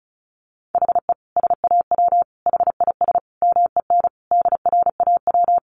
Morse code, heard as a keyed tone, about 34 wpm, 700 Hz. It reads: HE SAW HIS MEN DRAW
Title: Sentences from Top 300 Words 35wpm